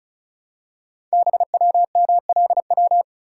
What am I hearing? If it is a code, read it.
Morse code, heard as a keyed tone, about 36 wpm, 700 Hz. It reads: BWMLW
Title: Code Groups - 5 Letters - 35wpm